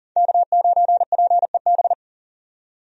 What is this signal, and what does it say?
Morse code, heard as a keyed tone, about 40 wpm, 700 Hz. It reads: K9PEB
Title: USA Call Signs - 5 Characters - 40wpm